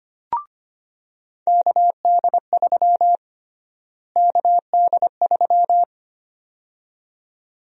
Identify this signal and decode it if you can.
Morse code, heard as a keyed tone, about 25 wpm, 700 Hz. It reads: KD3 KD3
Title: USA Call Sign Prefixes - Repeated 2x - 25wpm